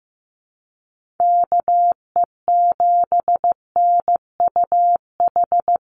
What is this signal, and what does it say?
Morse code, heard as a keyed tone, about 15 wpm, 700 Hz. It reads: KE7NUH
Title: USA Call Signs Repeated 3x 15wpm